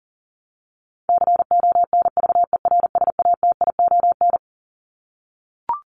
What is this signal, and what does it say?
Morse code, heard as a keyed tone, about 40 wpm, 700 Hz. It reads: CONVERSATION
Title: Top 1751-2000 Words 40wpm